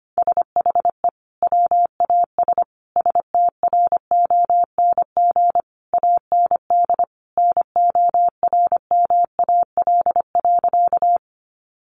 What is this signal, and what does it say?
Morse code, heard as a keyed tone, about 25 wpm, 700 Hz. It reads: SHE WAS STRONG AND NORMAL.
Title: Sentences from Top 1200 Words 25wpm